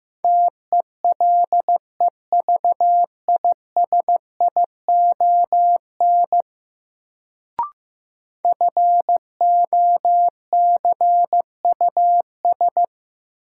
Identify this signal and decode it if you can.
Morse code, heard as a keyed tone, about 15 wpm, 700 Hz. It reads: TELEVISION FOCUS
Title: Sentences from Top 1000 Words - Review Part 4 - New Word Drill - 15wpm